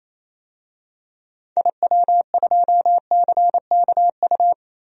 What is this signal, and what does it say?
Morse code, heard as a keyed tone, about 28 wpm, 700 Hz. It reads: IW2CKU
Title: DXCC-100 Call Signs - 6 Characters - Repeated 2x - 28wpm